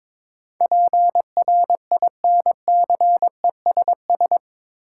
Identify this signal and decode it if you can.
Morse code, heard as a keyed tone, about 22 wpm, 700 Hz. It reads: PRINCESS
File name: practice / 22wpm / Eight Letter Words - 22wpm